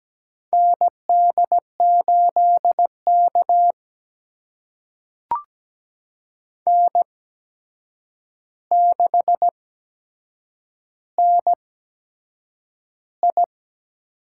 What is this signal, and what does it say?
Morse code, heard as a keyed tone, about 17 wpm, spaced as for 7 wpm, 700 Hz. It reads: ND8K N6NI
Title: USA Call Signs - 4 Characters - 8x Character Spacing - 17wpm